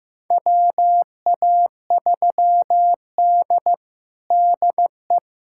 Morse code, heard as a keyed tone, about 15 wpm, 700 Hz. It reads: WA3D DE
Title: USA Callsign DE Callsign - 4 Characters - 15wpm